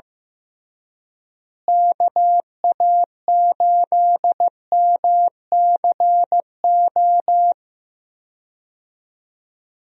Morse code, heard as a keyed tone, about 16 wpm, 700 Hz. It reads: KA8MCO
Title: USA Call Signs - 6 Characters - Repeated 3x - 15wpm